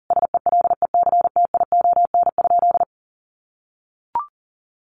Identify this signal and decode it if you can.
Morse code, heard as a keyed tone, about 40 wpm, 700 Hz. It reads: SELECTION?
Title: Sentences from Top 1400 Words 40wpm